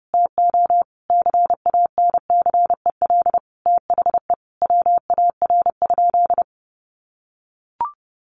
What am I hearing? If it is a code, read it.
Morse code, heard as a keyed tone, about 30 wpm, 700 Hz. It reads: TO CANCEL THE WAR?
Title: Sentences from Top 1200 Words 30wpm